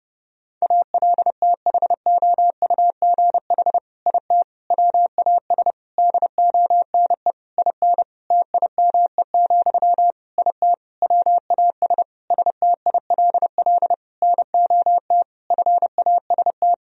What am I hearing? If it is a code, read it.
Morse code, heard as a keyed tone, about 30 wpm, 700 Hz. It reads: ALTHOUGH IT WAS DONE IN TIME, IT WAS STILL NOT FAST
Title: Sentences from Top 1000 Words - Review Part 3 - 30wpm